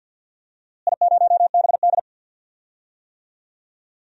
Morse code, heard as a keyed tone, about 45 wpm, 700 Hz. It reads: I0BD
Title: DXCC-100 Call Signs - 4 Characters - Repeated 2x - 50wpm